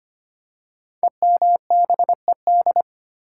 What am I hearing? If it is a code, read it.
Morse code, heard as a keyed tone, about 25 wpm, 700 Hz. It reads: EMBED
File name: 5 Letter Words - First 3 Letters Given - 25wpm